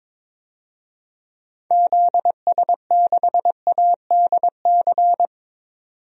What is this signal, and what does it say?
Morse code, heard as a keyed tone, about 22 wpm, 700 Hz. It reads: ZS6ADC
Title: DXCC-100 Call Signs - 6 Characters - 22wpm